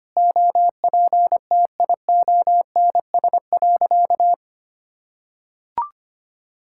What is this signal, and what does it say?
Morse code, heard as a keyed tone, about 25 wpm, 700 Hz. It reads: OPTIONS.
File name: practice / 25wpm / Sentences with 3-Word Phrases - 25wpm